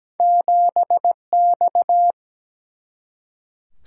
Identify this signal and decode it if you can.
Morse code, heard as a keyed tone, about 17 wpm, 700 Hz. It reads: 7X
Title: Single-Letter Single-Number 17wpm